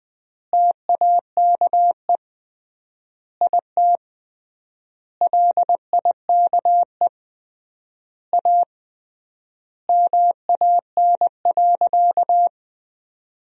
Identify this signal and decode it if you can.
Morse code, heard as a keyed tone, about 20 wpm, 700 Hz. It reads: TAKE IT LIKE A MAN.
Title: 3x Word-Spacing - Sentences from Top 300 Words - 20wpm